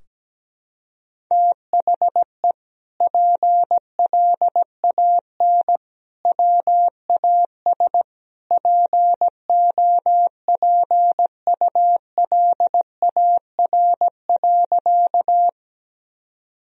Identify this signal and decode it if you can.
Morse code, heard as a keyed tone, about 17 wpm, 700 Hz. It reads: THE PLAN WAS POPULAR.
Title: Sentences from Top 1000 Words - Review Part 3 - 17wpm